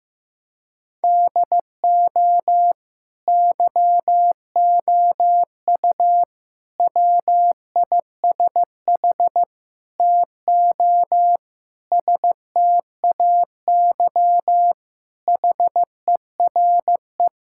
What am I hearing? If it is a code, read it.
Morse code, heard as a keyed tone, about 15 wpm, 700 Hz. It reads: DO YOU WISH TO STAY HERE
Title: Sentences from Top 1600 Words 15wpm